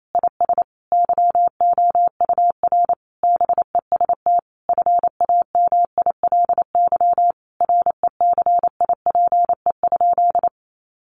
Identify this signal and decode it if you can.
Morse code, heard as a keyed tone, about 28 wpm, 700 Hz. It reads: IS YOUR BEST FAMILY RECIPE?
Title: Sentences from Top 1200 Words 28wpm